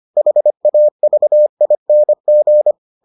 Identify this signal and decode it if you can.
Morse code, heard as a keyed tone, about 25 wpm, 600 Hz. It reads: HAVING